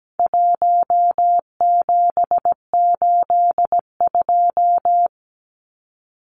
Morse code, heard as a keyed tone, about 17 wpm, 700 Hz. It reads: 1782
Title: Numbers - 4-Digit 17wpm